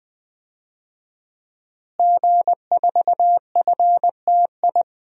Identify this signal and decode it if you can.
Morse code, heard as a keyed tone, about 20 wpm, 700 Hz. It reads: G4FTI
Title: DXCC-100 Call Signs - 5 Characters - Repeated 3x - 20wpm